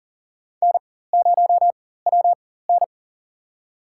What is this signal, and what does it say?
Morse code, heard as a keyed tone, about 40 wpm, spaced as for 19 wpm, 700 Hz. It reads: N0WN
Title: USA Call Signs - 4 Characters - 4x Character Spacing - 40wpm